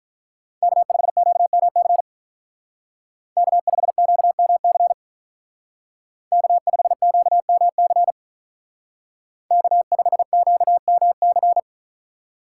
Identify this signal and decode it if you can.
Morse code, heard as a keyed tone, about 46 wpm, 700 Hz. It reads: K5QMC K5QMC K5QMC K5QMC